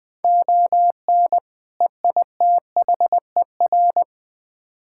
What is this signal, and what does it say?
Morse code, heard as a keyed tone, about 20 wpm, 700 Hz. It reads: ON EITHER
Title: Build A Phrase - 5-Word Phrases - 20wpm